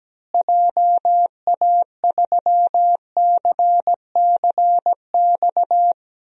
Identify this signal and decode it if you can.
Morse code, heard as a keyed tone, about 17 wpm, 700 Hz. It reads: JA3CCX
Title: DXCC-100 Call Signs - 6 Characters - Repeated 3x - 17wpm